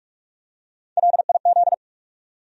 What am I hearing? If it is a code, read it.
Morse code, heard as a keyed tone, about 42 wpm, 700 Hz. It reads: LIZ
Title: USA Common Names - Max 4 Letters - 45wpm